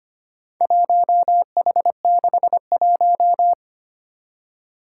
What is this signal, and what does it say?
Morse code, heard as a keyed tone, about 25 wpm, 700 Hz. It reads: 1H61
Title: Four Challenging Letter-Number Combinations 25wpm